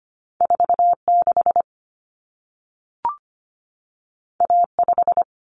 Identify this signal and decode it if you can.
Morse code, heard as a keyed tone, about 25 wpm, 700 Hz. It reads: 46 A5
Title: Common Cut Numbers - 2-Digit 25wpm